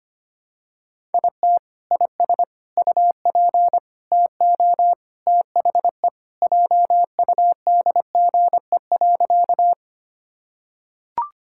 Morse code, heard as a keyed tone, about 25 wpm, 700 Hz. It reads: IT IS UP TO THE JUDGE.
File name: practice / 25wpm / Sentences with 3-Word Phrases - 25wpm